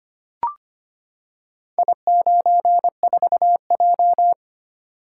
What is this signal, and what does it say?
Morse code, heard as a keyed tone, about 25 wpm, 700 Hz. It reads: I94J
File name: practice / 25wpm / Four Letter-Number Combinations - 25wpm